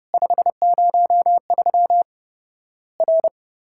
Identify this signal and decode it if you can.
Morse code, heard as a keyed tone, about 30 wpm, 700 Hz. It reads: 503 R
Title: Worked All Europe DX Contest - QTC Exchange - 30wpm